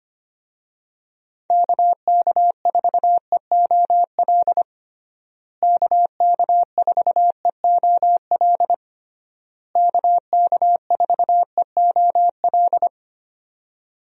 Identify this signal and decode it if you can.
Morse code, heard as a keyed tone, about 25 wpm, 700 Hz. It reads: KK4EOL KK4EOL KK4EOL
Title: USA Call Signs - 6 Characters - Repeated 3x - 25wpm